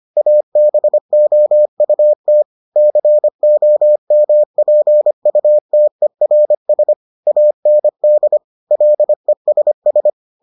Morse code, heard as a keyed tone, about 25 wpm, 600 Hz. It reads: ABOUT COMPUTERS AND LESS